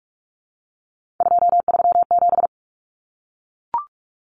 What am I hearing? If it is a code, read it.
Morse code, heard as a keyed tone, about 44 wpm, 700 Hz. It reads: 237
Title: Numbers - 3-Digit 45wpm